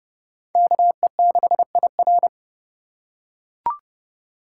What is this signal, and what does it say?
Morse code, heard as a keyed tone, about 32 wpm, 700 Hz. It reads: KE6IR
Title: USA Call Signs - 2x Character Spacing - 30wpm